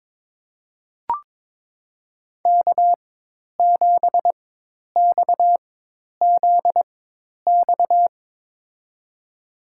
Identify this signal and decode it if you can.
Morse code, heard as a keyed tone, about 22 wpm, spaced as for 10 wpm, 700 Hz. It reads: K7XZX
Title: USA Call Signs - 5 Characters - 4x Character Spacing - 22wpm